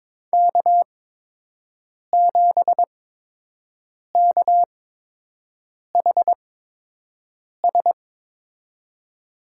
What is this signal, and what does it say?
Morse code, heard as a keyed tone, about 22 wpm, spaced as for 6 wpm, 700 Hz. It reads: K7KHS
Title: USA Call Signs - 8x Character Spacing - 22wpm